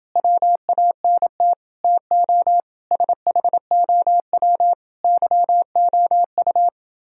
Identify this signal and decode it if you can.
Morse code, heard as a keyed tone, about 29 wpm, 700 Hz. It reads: WANT TO SHOW YOU